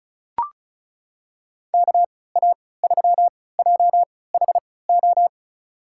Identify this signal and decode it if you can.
Morse code, heard as a keyed tone, about 36 wpm, spaced as for 20 wpm, 700 Hz. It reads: KA3JHO
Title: USA Call Signs - 3x Character Spacing - 35wpm